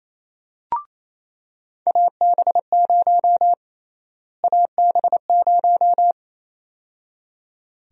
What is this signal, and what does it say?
Morse code, heard as a keyed tone, about 28 wpm, 700 Hz. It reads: AB0 AB0
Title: USA Call Sign Prefixes - Repeated 2x - 28wpm